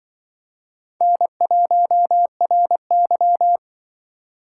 Morse code, heard as a keyed tone, about 25 wpm, 700 Hz. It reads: N1RY
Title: Speed Racing - USA 1x2 Call Signs 20wpm